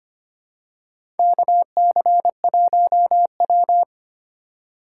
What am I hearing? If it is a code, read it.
Morse code, heard as a keyed tone, about 25 wpm, 700 Hz. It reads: KC1W
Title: USA Call Signs - 4 Characters - 3x Character Spacing - 25wpm